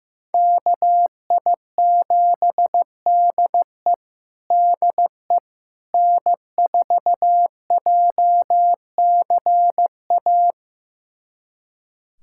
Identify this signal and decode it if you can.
Morse code, heard as a keyed tone, about 15 wpm, 700 Hz. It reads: KI7DE DE N4JCA
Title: USA Callsign DE Callsign - 5 Characters - 15wpm